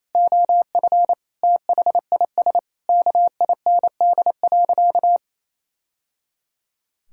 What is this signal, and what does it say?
Morse code, heard as a keyed tone, about 28 wpm, 700 Hz. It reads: OF THIS KIND.
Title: Sentences from Top 1000 Words - Review Part 5 - 28wpm